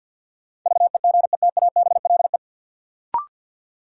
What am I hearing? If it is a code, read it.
Morse code, heard as a keyed tone, about 53 wpm, 700 Hz. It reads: VEGETABLE
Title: Sentences from Top 1300 Words - New Word Drill - 50wpm